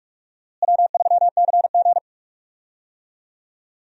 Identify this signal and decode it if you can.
Morse code, heard as a keyed tone, about 44 wpm, 700 Hz. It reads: W3CG
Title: USA Call Signs - 4 Characters - Repeated 3x - 45wpm